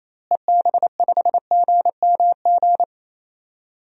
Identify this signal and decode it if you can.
Morse code, heard as a keyed tone, about 28 wpm, 700 Hz. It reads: EB5GMG
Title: DXCC-100 Call Signs - 6 Characters - Repeated 3x - 28wpm